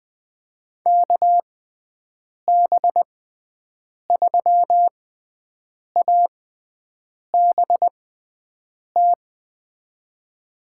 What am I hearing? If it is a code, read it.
Morse code, heard as a keyed tone, about 20 wpm, spaced as for 7 wpm, 700 Hz. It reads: KB3ABT
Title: USA Call Signs - 6 Characters - 6x Character Spacing - 20wpm